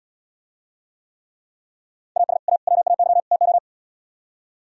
Morse code, heard as a keyed tone, about 77 wpm, 700 Hz. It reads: AS A WAY TO